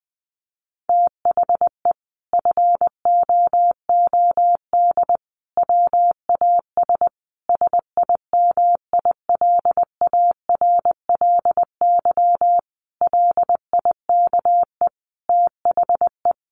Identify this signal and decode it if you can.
Morse code, heard as a keyed tone, about 20 wpm, 700 Hz. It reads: THE FOOD WAS SIMILARLY LIKE THE
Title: Sentences from Top 1400 Words 20wpm